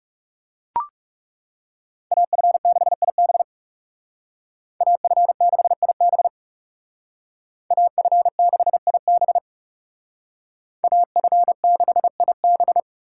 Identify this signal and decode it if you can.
Morse code, heard as a keyed tone, about 34 wpm, 700 Hz. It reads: AF6IB AF6IB AF6IB AF6IB
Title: Speed Racing - USA 2x2 Call Signs 30wpm